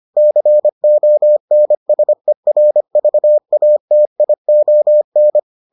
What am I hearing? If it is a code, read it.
Morse code, heard as a keyed tone, about 25 wpm, 600 Hz. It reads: CONSERVATION